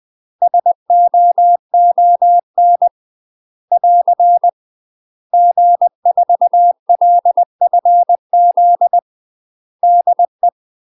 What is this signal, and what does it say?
Morse code, heard as a keyed tone, about 20 wpm, 700 Hz. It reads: SOON <AR> G4LFZ DE
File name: UK MASTER QSOs TRAINING / 20WPM / UK MASTER QSOs 0066